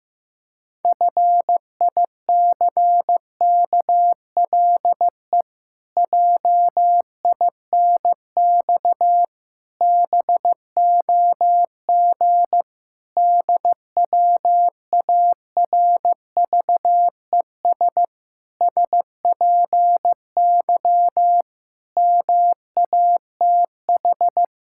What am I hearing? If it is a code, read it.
Morse code, heard as a keyed tone, about 15 wpm, 700 Hz. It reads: FICKLE JINX BOG DWARVES SPY MATH
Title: Pangram Challenge v2 - 15wpm